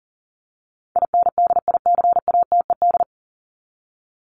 Morse code, heard as a keyed tone, about 40 wpm, 700 Hz. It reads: INDICATED
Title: ED Suffix - Top 100 Words with Base Words 40wpm